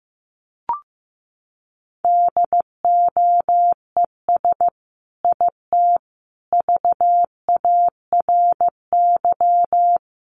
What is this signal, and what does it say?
Morse code, heard as a keyed tone, about 15 wpm, 700 Hz. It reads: DOES IT VARY
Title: Sentences from Top 1200 Words 15wpm